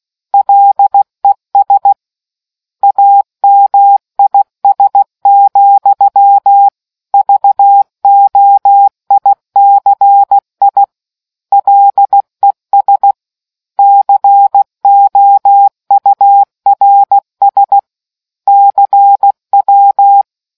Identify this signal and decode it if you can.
Morse code, heard as a keyed tone, about 16 wpm, 800 Hz. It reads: LES AMIS, VOICI LES COURS CW